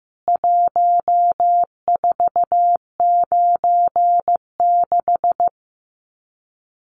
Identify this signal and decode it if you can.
Morse code, heard as a keyed tone, about 15 wpm, 700 Hz. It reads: 1496
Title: CWT Exchange 15wpm